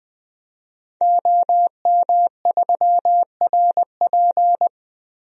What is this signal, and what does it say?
Morse code, heard as a keyed tone, about 20 wpm, 700 Hz. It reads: OM3RP
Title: DXCC-100 Call Signs - 5 Characters - Repeated 3x - 20wpm